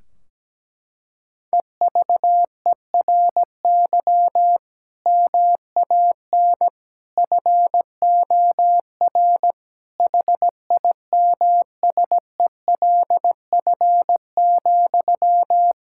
Morse code, heard as a keyed tone, about 17 wpm, 700 Hz. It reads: EVERY MAN FOR HIMSELF,
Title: Sentences from Top 1600 Words 17wpm